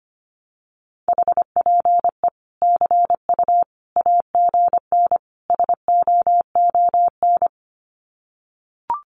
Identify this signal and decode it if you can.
Morse code, heard as a keyed tone, about 25 wpm, 700 Hz. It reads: HPE CU AGN SOON
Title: Common QSO Phrases 25wpm